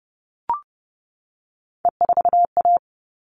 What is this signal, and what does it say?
Morse code, heard as a keyed tone, about 30 wpm, 700 Hz. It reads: E4A
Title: All Cut Numbers - 3-Digit 30wpm